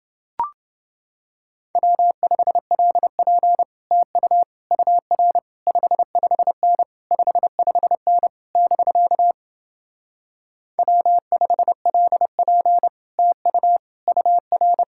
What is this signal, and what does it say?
Morse code, heard as a keyed tone, about 31 wpm, 700 Hz. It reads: W5LP TU UR 55N 55N <BK> W5LP TU UR